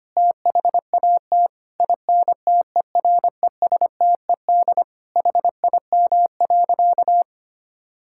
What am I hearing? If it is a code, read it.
Morse code, heard as a keyed tone, about 25 wpm, 700 Hz. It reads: THAT INTERESTED HIM.
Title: Sentences from Top 1000 Words - Review Part 6 - 25wpm